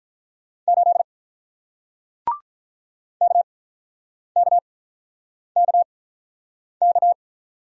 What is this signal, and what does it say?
Morse code, heard as a keyed tone, about 48 wpm, spaced as for 9 wpm, 700 Hz. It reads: 8 KKKK